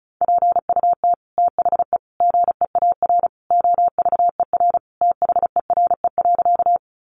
Morse code, heard as a keyed tone, about 34 wpm, 700 Hz. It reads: PUT THE GEAR OVER THERE.
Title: Sentences from Top 1400 Words 35wpm